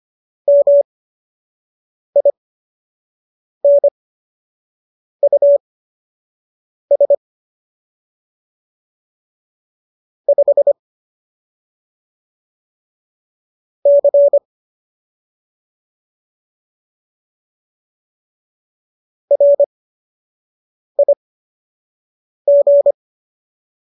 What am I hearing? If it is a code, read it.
Morse code, heard as a keyed tone, about 25 wpm, spaced as for 6 wpm, 570 Hz. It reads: MINUS 5 C RIG
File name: qso002 06